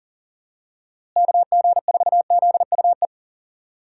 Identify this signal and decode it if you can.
Morse code, heard as a keyed tone, about 40 wpm, 700 Hz. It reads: KG4ZUE
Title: USA Call Signs - 6 Characters - Repeated 2x - 40wpm